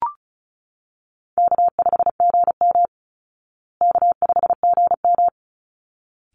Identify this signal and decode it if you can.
Morse code, heard as a keyed tone, about 34 wpm, 700 Hz. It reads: K5GM K5GM
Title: Call Signs Repeated 3x - 35wpm